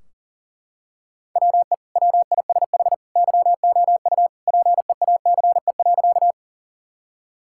Morse code, heard as a keyed tone, about 40 wpm, 700 Hz. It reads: WE WISH YOU PEACE.
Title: Sentences from Top 1600 Words 40wpm